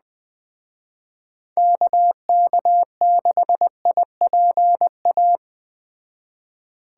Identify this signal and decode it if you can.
Morse code, heard as a keyed tone, about 21 wpm, 700 Hz. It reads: KK6IPA
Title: Speed Racing - USA 2x3 Call Signs 17wpm